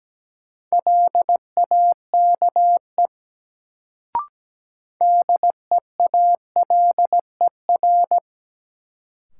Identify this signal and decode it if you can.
Morse code, heard as a keyed tone, about 17 wpm, 700 Hz. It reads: LAKE DEALER
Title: Sentences from Top 1100 Words - New Word Drill Alt - 17wpm